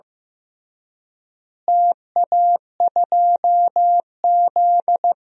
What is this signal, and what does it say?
Morse code, heard as a keyed tone, about 15 wpm, 700 Hz. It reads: TA2Z
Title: DXCC-100 Call Signs - 4 Characters - Repeated 3x - 15wpm